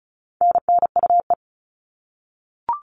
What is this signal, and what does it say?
Morse code, heard as a keyed tone, about 34 wpm, 700 Hz. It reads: NNUE
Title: All Cut Numbers - 4-Digit 35wpm